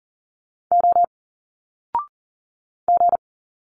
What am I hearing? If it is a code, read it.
Morse code, heard as a keyed tone, about 39 wpm, 700 Hz. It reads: O G
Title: Single Letters 40wpm